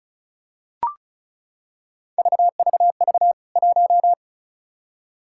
Morse code, heard as a keyed tone, about 36 wpm, 700 Hz. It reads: VVV 1
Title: K1USN SST QSO Practice 35wpm